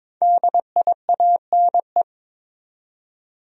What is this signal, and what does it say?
Morse code, heard as a keyed tone, about 21 wpm, 700 Hz. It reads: DIANE
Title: USA Common Names - 22wpm